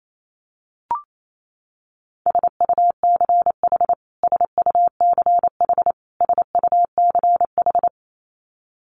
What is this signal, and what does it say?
Morse code, heard as a keyed tone, about 28 wpm, 700 Hz. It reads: SUCH SUCH SUCH
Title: Top 500 Words Repeated 28wpm